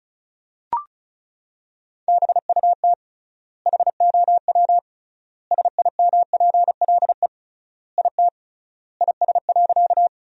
Morse code, heard as a keyed tone, about 36 wpm, 700 Hz. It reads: BUT HOW SIMPLE IT IS.
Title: 3x Word-Spacing - Sentences from Top 400 Words - 35wpm